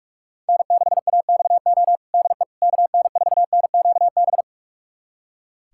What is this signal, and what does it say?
Morse code, heard as a keyed tone, about 46 wpm, 700 Hz. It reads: N6AXO DE KN4NQB
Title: USA Callsign DE Callsign 45wpm